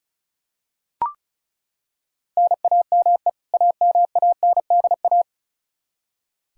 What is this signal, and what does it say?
Morse code, heard as a keyed tone, about 36 wpm, 700 Hz. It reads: NAME AMANDA
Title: Practice QSO - Exchange 1 - Beginning - 35wpm